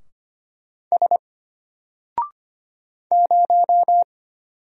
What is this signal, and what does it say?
Morse code, heard as a keyed tone, about 25 wpm, 700 Hz. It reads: S 0
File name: Single-Letter-Number-25wpm